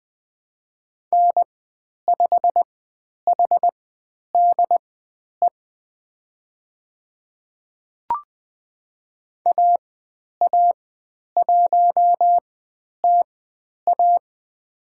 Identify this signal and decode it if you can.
Morse code, heard as a keyed tone, about 20 wpm, spaced as for 10 wpm, 700 Hz. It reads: N5HDE AA1TA
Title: 109 Lesson 18g - Letter m - Call Signs 20wpm